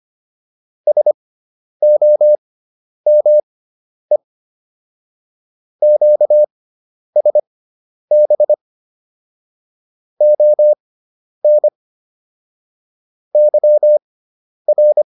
Morse code, heard as a keyed tone, about 25 wpm, spaced as for 10 wpm, 600 Hz. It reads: SOME QSB ON YR
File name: QSO301 10